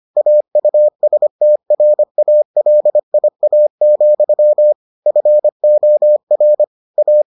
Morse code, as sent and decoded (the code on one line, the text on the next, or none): .- ..- ... - .-. .- .-.. .. .- --..-- / ..-. --- .-. / .-
AUSTRALIA, FOR A